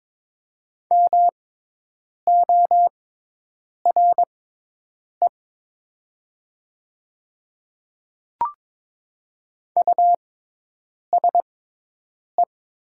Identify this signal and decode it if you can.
Morse code, heard as a keyed tone, about 22 wpm, spaced as for 8 wpm, 700 Hz. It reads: MORE USE
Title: 6x Character Spacing - Top 100 Words - 22wpm